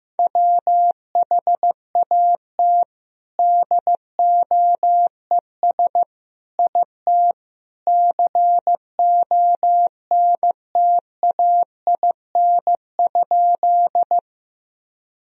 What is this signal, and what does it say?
Morse code, heard as a keyed tone, about 15 wpm, 700 Hz. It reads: WHAT DOES IT CONTAIN?
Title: Sentences from Top 1000 Words - Review Part 2 - 15wpm